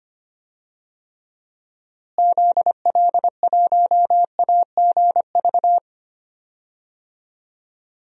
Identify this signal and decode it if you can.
Morse code, heard as a keyed tone, about 25 wpm, 700 Hz. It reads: ZL1AGV
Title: DXCC-100 Call Signs - 6 Characters - Repeated 3x - 25wpm